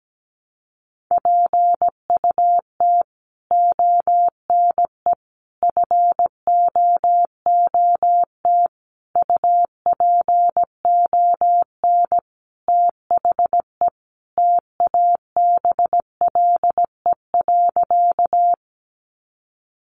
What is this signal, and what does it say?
Morse code, heard as a keyed tone, about 17 wpm, 700 Hz. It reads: PUT ONE FOOT UPON THE TABLE.
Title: Sentences from Top 1100 Words 17wpm